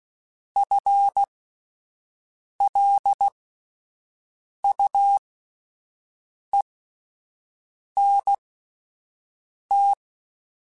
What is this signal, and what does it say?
Morse code, heard as a keyed tone, about 16 wpm, spaced as for 5 wpm, 800 Hz. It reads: FLUENT
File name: Spaced / FL Runs sp mp3s / FL Runs sp @10